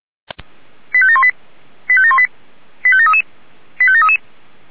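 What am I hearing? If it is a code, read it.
Five-tone selective call (ZVEI), heard as quick runs of five tones, tones of about 70 ms. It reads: 85718 85718 85733 85733
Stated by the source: ZVEI five-tone selective call 89135 received by SDR, from the Feuerwehr Orlamünde audio archive.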